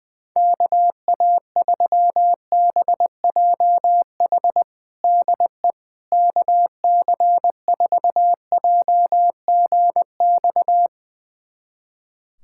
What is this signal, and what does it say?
Morse code, heard as a keyed tone, about 20 wpm, 700 Hz. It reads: KA3BJH DE KC4JGX
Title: USA Callsign DE Callsign - 6 Characters - 20wpm